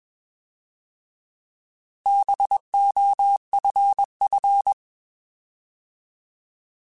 Morse code, heard as a keyed tone, about 21 wpm, 800 Hz. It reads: BOFF
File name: Bs @20